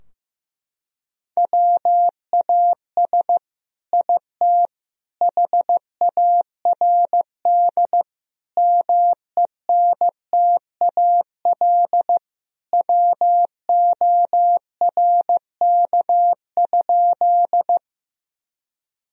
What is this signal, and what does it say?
Morse code, heard as a keyed tone, about 16 wpm, 700 Hz. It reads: WAS IT HARD MENTAL WORK?
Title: Sentences from Top 1000 Words - Review Part 4 - 15wpm